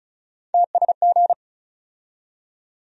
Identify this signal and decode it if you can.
Morse code, heard as a keyed tone, about 34 wpm, 700 Hz. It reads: TSG
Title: USA Call Sign Postfixes - 3 Letters 35wpm